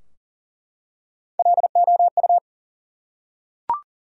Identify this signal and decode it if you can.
Morse code, heard as a keyed tone, about 40 wpm, 700 Hz. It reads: LOU
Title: USA Common Names - Max 3 Letters - 40wpm